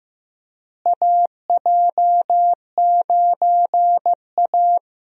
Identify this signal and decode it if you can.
Morse code, heard as a keyed tone, about 15 wpm, 700 Hz. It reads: AJ9A
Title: USA Call Signs - 4 Characters - Repeated 3x - 15wpm